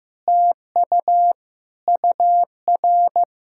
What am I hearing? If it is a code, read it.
Morse code, heard as a keyed tone, about 15 wpm, 700 Hz. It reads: TU UR
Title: SOTA QSO Practice 15wpm